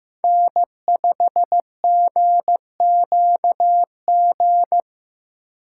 Morse code, heard as a keyed tone, about 16 wpm, 700 Hz. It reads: N5GQG
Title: Speed Racing - USA 1x3 Call Signs 15wpm